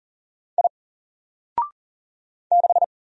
Morse code, heard as a keyed tone, about 39 wpm, 700 Hz. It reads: I 6
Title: Single Character (Letter:Number:Punctuation:Prosign) 40wpm